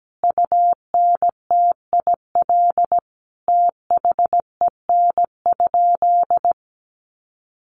Morse code, heard as a keyed tone, about 17 wpm, 700 Hz. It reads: UNTIL THEN?
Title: Sentences from Top 500 Words 17wpm